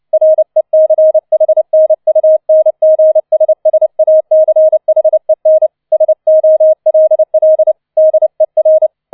Morse code, heard as a keyed tone, about 29 wpm, 630 Hz. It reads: RECHNUNGSSACHEN SOLL DER
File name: 004 Vierter Streich 30wpm